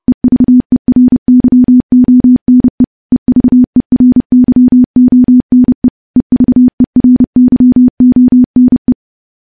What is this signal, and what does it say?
Morse code, heard as a keyed tone, about 30 wpm, 250 Hz. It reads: EVERYONE EVERYONE EVERYONE
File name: morse / morse-training-30WPM